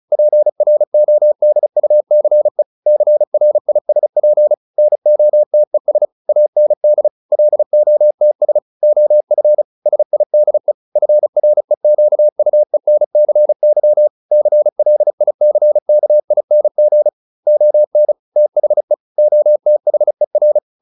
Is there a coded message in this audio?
Yes: Morse code from 0.1 to 20.6 s, heard as a keyed tone, about 34 wpm, 600 Hz. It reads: PRODUCE CRISP NOTES AND LOTS OF SIDE FREQUENCY CLICKING ON THE OTHER